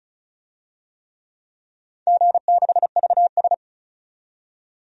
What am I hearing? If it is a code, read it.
Morse code, heard as a keyed tone, about 34 wpm, 700 Hz. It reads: G6VS
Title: DXCC-100 Call Signs - 4 Characters - 35wpm